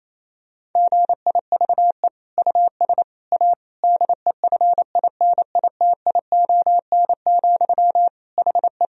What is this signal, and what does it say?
Morse code, heard as a keyed tone, about 28 wpm, 700 Hz. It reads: GIVE US A DEFINITION, HE
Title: Sentences from Top 1000 Words - Review Part 4 - 28wpm